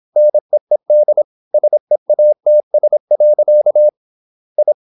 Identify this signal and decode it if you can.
Morse code, heard as a keyed tone, about 26 wpm, 600 Hz. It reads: NEED SEATS. I
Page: Never 26wpm | CW med Gnister